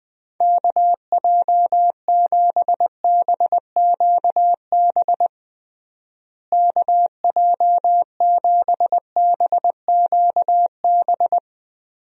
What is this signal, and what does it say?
Morse code, heard as a keyed tone, about 20 wpm, 700 Hz. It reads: KJ7BQB KJ7BQB
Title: USA Call Signs - 6 Characters - Repeated 3x - 20wpm